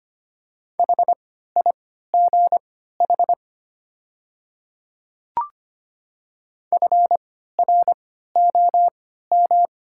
Morse code, heard as a keyed tone, about 25 wpm, spaced as for 14 wpm, 690 Hz. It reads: HIGH FROM
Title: 3x Character Spacing - Top 500 Words - 25wpm